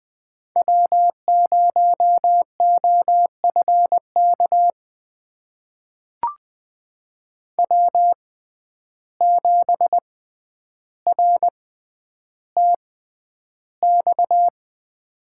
Morse code, heard as a keyed tone, about 20 wpm, spaced as for 10 wpm, 700 Hz. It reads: W0OFK W7RTX
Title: USA Call Signs - 5 Characters - 6x Character Spacing - 20wpm